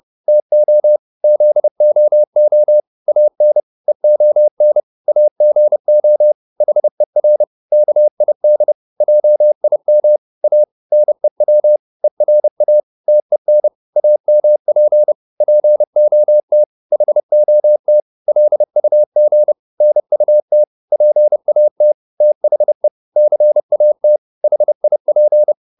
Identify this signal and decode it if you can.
Morse code, heard as a keyed tone, about 30 wpm, 600 Hz. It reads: TO ZOO AN EON AGO HER KID JIM A NEW ERA TEN AMP POT HOT LUG NUT PAT THE CAT HIP